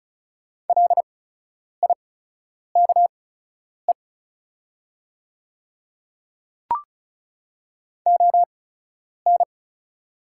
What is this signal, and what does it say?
Morse code, heard as a keyed tone, about 36 wpm, spaced as for 10 wpm, 700 Hz. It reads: LIKE ON
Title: 8x Character Spacing - Top 100 Words - 35wpm